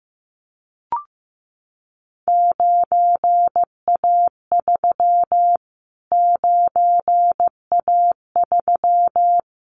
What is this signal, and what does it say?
Morse code, heard as a keyed tone, about 15 wpm, 700 Hz. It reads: 9A3 9A3
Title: Callsign Prefixes for Top 25 DXCC - Repeated 3x - 15wpm